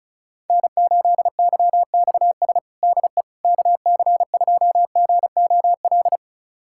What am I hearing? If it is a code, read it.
Morse code, heard as a keyed tone, about 36 wpm, 700 Hz. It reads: N8YXS DE KC2GOL
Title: USA Callsign DE Callsign 35wpm